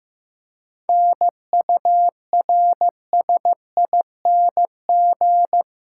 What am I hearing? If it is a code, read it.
Morse code, heard as a keyed tone, about 15 wpm, 700 Hz. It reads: NURSING
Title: Practice QSO - Exchange 3 - More Info - 15wpm